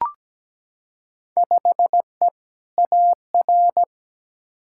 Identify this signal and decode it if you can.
Morse code, heard as a keyed tone, about 17 wpm, 700 Hz. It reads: 5E AR
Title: Field Day Exchanges Encore - 17wpm